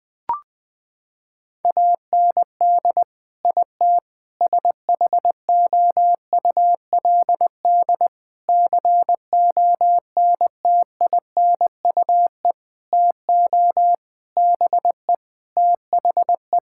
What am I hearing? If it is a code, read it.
Morse code, heard as a keyed tone, about 21 wpm, 700 Hz. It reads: AND IT SHOULD CONTINUE TO BE THE
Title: Sentences from Top 1000 Words - Review Part 6 - 20wpm